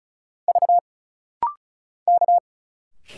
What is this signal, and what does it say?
Morse code, heard as a keyed tone, about 36 wpm, 700 Hz. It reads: V K
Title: Single Letters - Rapid-Fire 35wpm